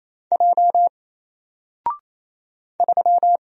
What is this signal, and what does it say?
Morse code, heard as a keyed tone, about 27 wpm, 700 Hz. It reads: J 3
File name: Single-Letter-Number-28wpm